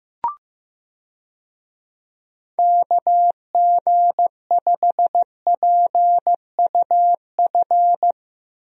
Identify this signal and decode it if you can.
Morse code, heard as a keyed tone, about 15 wpm, 700 Hz. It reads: KG5PUF
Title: USA Call Signs - 6 Characters - Repeated 2x - 15wpm